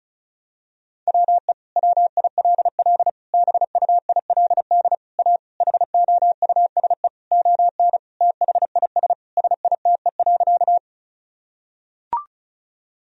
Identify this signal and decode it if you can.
Morse code, heard as a keyed tone, about 36 wpm, 700 Hz. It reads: WE WILL BUILD A HOUSE ON THIS SITE.
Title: Sentences from Top 1000 Words - Review Part 4 - 35wpm